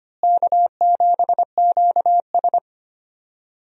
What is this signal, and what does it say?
Morse code, heard as a keyed tone, about 25 wpm, 700 Hz. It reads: K7QS
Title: USA Call Signs - 4 Characters - Repeated 3x - 25wpm